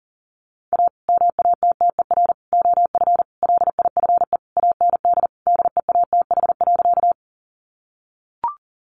Morse code, heard as a keyed tone, about 40 wpm, 700 Hz. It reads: A MATTER OF LIFE AND DEATH.
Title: Sentences from Top 1100 Words 40wpm